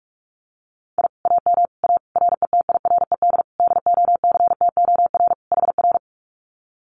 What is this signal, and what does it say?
Morse code, heard as a keyed tone, about 44 wpm, 700 Hz. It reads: I AM A RETIRED DOCTOR HR
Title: Common QSO Phrases Encore 45wpm